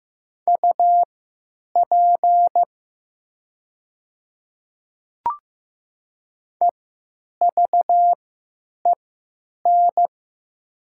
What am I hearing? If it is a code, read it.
Morse code, heard as a keyed tone, about 15 wpm, spaced as for 8 wpm, 700 Hz. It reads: UP EVEN
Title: 3x Character Spacing - Top 500 Words - 15wpm